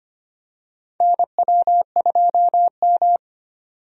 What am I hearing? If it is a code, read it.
Morse code, heard as a keyed tone, about 25 wpm, 700 Hz. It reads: NW2M
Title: USA Call Signs - 4 Characters - 25wpm